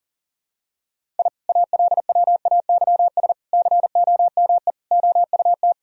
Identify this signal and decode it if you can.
Morse code, heard as a keyed tone, about 37 wpm, 700 Hz. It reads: I ALWAYS COME OUT
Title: Sentences from Top 1000 Words - Review Part 2 - 40wpm